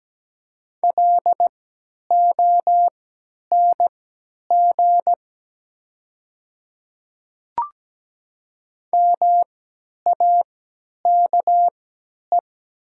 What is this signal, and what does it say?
Morse code, heard as a keyed tone, about 17 wpm, spaced as for 10 wpm, 700 Hz. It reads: LONG MAKE
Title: 3x Character Spacing - Top 100 Words - 17wpm